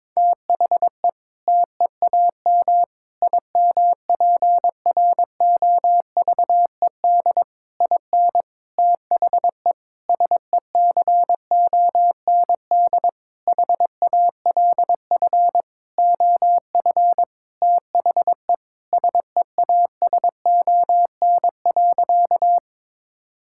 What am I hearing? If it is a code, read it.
Morse code, heard as a keyed tone, about 22 wpm, 700 Hz. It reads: THE TEAM IMPROVED IN THE SECOND HALF OF THE SEASON.
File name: Sentences with 7-Word Phrases - 22wpm